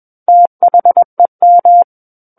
Morse code, heard as a keyed tone, about 21 wpm, 700 Hz. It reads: THEM
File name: Book One / 12-Chapter 12 - 21 WPM